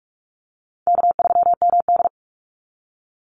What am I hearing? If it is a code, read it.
Morse code, heard as a keyed tone, about 44 wpm, 700 Hz. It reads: K3MD
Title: Easy Call Signs Repeated 3x - 45wpm